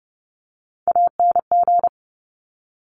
Morse code, heard as a keyed tone, about 30 wpm, 700 Hz. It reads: ANG
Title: Top Three Letter Combinations in Words 30wpm